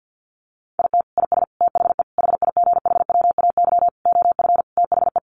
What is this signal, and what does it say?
Morse code, heard as a keyed tone, about 50 wpm, 700 Hz. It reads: IT IS THE HIGHWAY OF THE
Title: Sentences from Top 1400 Words 50wpm